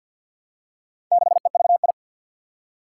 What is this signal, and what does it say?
Morse code, heard as a keyed tone, about 50 wpm, 700 Hz. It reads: 6EVI
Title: Four Challenging Letter-Number Combinations 50wpm